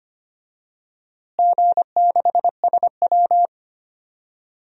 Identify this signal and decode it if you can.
Morse code, heard as a keyed tone, about 25 wpm, 700 Hz. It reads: G6SW
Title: DXCC-100 Call Signs - 4 Characters - Repeated 2x - 25wpm